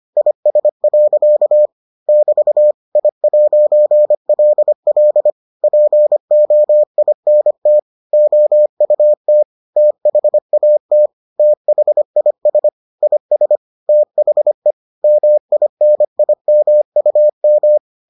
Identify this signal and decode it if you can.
Morse code, heard as a keyed tone, about 25 wpm, 600 Hz. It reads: IS. <BT> I'LL POINT OUT THAT THIS IS THE MINIMUM